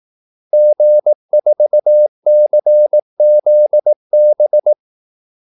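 Morse code, heard as a keyed tone, about 18 wpm, 600 Hz. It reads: G4CZB
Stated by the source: Morse call-sign practice at 18 wpm.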